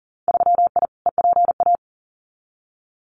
Morse code, heard as a keyed tone, about 40 wpm, 700 Hz. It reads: 3I EPA
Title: Winter Field Day Exchanges 40wpm